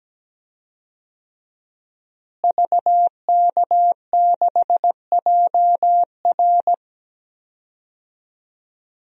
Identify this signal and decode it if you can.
Morse code, heard as a keyed tone, about 17 wpm, 700 Hz. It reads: VK6JR
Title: DXCC-100 Call Signs - 5 Characters - Repeated 2x - 17wpm